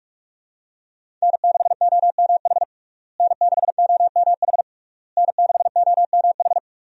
Morse code, heard as a keyed tone, about 44 wpm, 700 Hz. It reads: N6OMH N6OMH N6OMH
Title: USA Call Signs - 5 Characters - Repeated 3x - 45wpm